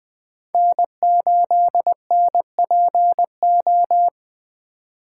Morse code, heard as a keyed tone, about 21 wpm, 700 Hz. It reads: N8NPO
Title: Speed Racing - USA 1x3 Call Signs 17wpm